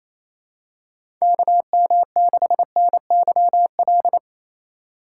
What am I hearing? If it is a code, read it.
Morse code, heard as a keyed tone, about 27 wpm, 700 Hz. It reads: KM6NYL
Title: USA Call Signs - 6 Characters - 3x Character Spacing - 28wpm